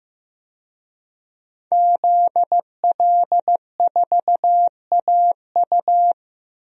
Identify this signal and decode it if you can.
Morse code, heard as a keyed tone, about 15 wpm, 700 Hz. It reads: ZL4AU
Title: DXCC-100 Call Signs - 5 Characters - Repeated 2x - 15wpm